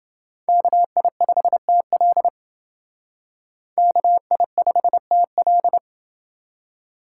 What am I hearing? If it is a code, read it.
Morse code, heard as a keyed tone, about 28 wpm, 700 Hz. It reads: KI5TL KI5TL